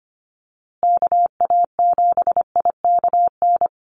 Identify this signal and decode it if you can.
Morse code, heard as a keyed tone, about 25 wpm, 700 Hz. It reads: KA7IKN
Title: USA Call Signs Repeated 2x 25wpm